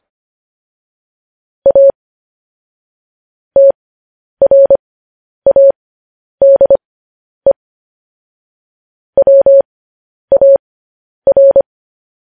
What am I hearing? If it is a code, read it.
Morse code, heard as a keyed tone, about 25 wpm, spaced as for 10 wpm, 550 Hz. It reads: A TRADE WAR
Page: news headlines in morse code at 10 wpm